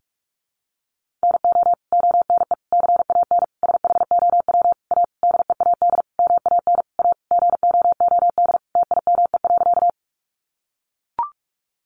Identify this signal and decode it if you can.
Morse code, heard as a keyed tone, about 44 wpm, 700 Hz. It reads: NO ONE CAN SHOW A DEAD MAN A GOOD TIME.
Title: Sentences from Top 1500 Words 45wpm